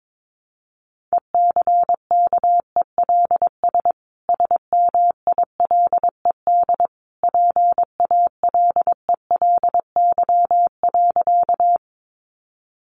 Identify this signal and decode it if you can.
Morse code, heard as a keyed tone, about 22 wpm, 700 Hz. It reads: ECKELS SMILED PALELY.
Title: A Sound Of Thunder 22wpm